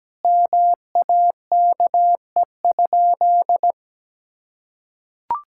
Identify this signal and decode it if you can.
Morse code, heard as a keyed tone, about 17 wpm, 700 Hz. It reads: MAKE?
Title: Sentences from Top 1000 Words - Review Part 5 - 17wpm